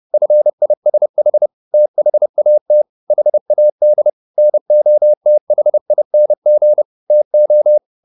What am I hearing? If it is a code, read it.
Morse code, heard as a keyed tone, about 30 wpm, 600 Hz. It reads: FISH THAT HAD NOTHING TO